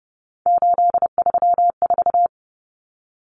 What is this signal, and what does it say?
Morse code, heard as a keyed tone, about 30 wpm, 700 Hz. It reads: 834
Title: Common Cut Numbers - 3-Digit 30wpm